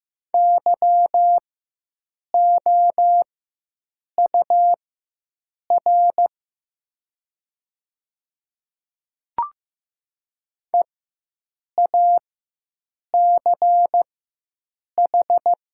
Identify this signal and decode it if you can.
Morse code, heard as a keyed tone, about 15 wpm, spaced as for 7 wpm, 700 Hz. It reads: YOUR EACH
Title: 4x Character Spacing - Top 100 Words - 15wpm